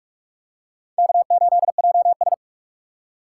Morse code, heard as a keyed tone, about 44 wpm, 700 Hz. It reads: K8JS